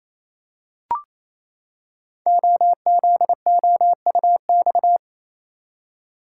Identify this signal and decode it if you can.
Morse code, heard as a keyed tone, about 28 wpm, 700 Hz. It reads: OZOUX
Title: Code Groups - 5 Letter:Numbers - 28wpm